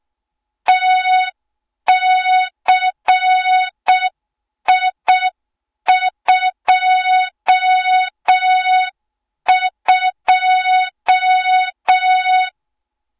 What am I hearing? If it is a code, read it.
Morse code, heard as a keyed tone, about 6 wpm, 3700 Hz. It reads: TCI22
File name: Morze-HA2MN / hang / 20 GYAKORLAT